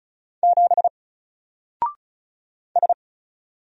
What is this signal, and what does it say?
Morse code, heard as a keyed tone, about 36 wpm, 700 Hz. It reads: 7 S
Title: Single Character (Letter:Number:Punctuation:Prosign) 35wpm